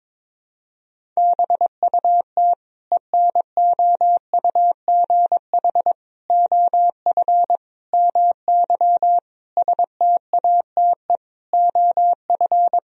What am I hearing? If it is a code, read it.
Morse code, heard as a keyed tone, about 22 wpm, 700 Hz. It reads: BUT ENOUGH OF MY STATE OF
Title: Sentences from Top 1000 Words - Review Part 2 - 22wpm